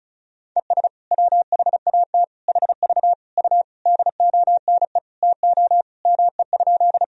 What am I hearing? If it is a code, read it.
Morse code, heard as a keyed tone, about 36 wpm, 700 Hz. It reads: ES WHAT HV U DONE TO ME?
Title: Sentences from Top 500 Words and Abbreviations 35wpm